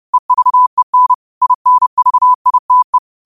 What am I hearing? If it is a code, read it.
Morse code, heard as a keyed tone, about 30 wpm, 1000 Hz. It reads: EVEN INVITE
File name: qotdfiles / 30wpm